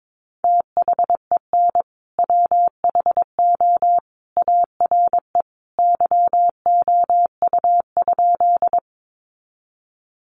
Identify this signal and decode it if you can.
Morse code, heard as a keyed tone, about 22 wpm, 700 Hz. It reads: THEN WHO ARE YOU?
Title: Sentences from Top 100 Words 22wpm